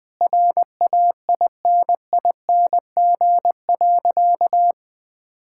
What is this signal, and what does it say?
Morse code, heard as a keyed tone, about 21 wpm, 700 Hz. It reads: RAINING.
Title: Sentences with 3-Word Phrases - 20wpm